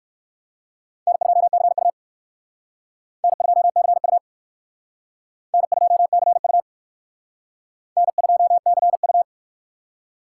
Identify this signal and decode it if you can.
Morse code, heard as a keyed tone, about 57 wpm, 700 Hz. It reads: N2CU N2CU N2CU N2CU